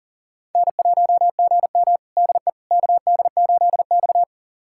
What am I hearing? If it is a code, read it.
Morse code, heard as a keyed tone, about 40 wpm, 700 Hz. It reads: N1GM DE KD8X
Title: USA Callsign DE Callsign - 4 Characters - 40wpm